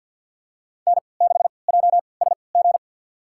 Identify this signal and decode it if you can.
Morse code, heard as a keyed tone, about 53 wpm, spaced as for 29 wpm, 700 Hz. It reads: N6JSG
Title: USA Call Signs - 5 Characters - 3x Character Spacing - 50wpm